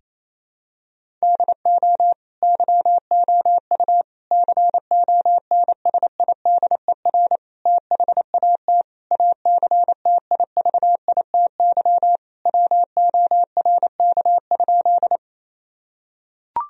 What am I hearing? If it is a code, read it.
Morse code, heard as a keyed tone, about 28 wpm, 700 Hz. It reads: DO YOU CONSIDER THAT ACTIVITY WORK?
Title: Sentences from Top 1000 Words - Review Part 4 - 28wpm